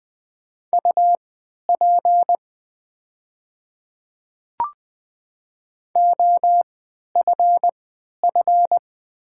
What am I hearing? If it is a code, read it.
Morse code, heard as a keyed tone, about 20 wpm, spaced as for 11 wpm, 700 Hz. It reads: UP OFF
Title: 3x Character Spacing - Top 200 Words - 20wpm